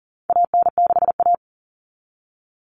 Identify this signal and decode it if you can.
Morse code, heard as a keyed tone, about 40 wpm, 700 Hz. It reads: AN6A
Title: Common Cut Numbers - 4-Digit 40wpm